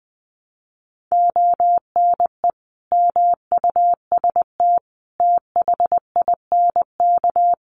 Morse code, heard as a keyed tone, about 20 wpm, 700 Hz. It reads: ONE MUST THINK
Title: Sentences from Top 700 Words 20wpm